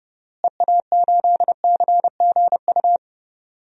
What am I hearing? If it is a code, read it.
Morse code, heard as a keyed tone, about 30 wpm, 700 Hz. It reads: EA8CGU
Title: DXCC-100 Call Signs - 6 Characters - Repeated 2x - 30wpm